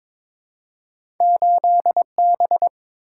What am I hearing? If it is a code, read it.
Morse code, heard as a keyed tone, about 22 wpm, 700 Hz. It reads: 8B
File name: Two Letter-Number Combinations - 22wpm